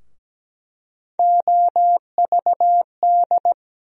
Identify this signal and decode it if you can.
Morse code, heard as a keyed tone, about 17 wpm, 700 Hz. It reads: OVD
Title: USA Call Sign Postfixes - 3 Letters 17wpm